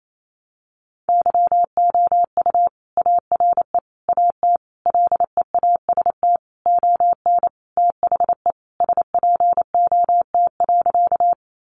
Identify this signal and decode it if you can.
Morse code, heard as a keyed tone, about 28 wpm, 700 Hz. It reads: YOU ARE AT LEAST ON THE SPOT.
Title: Sentences from Top 1100 Words 28wpm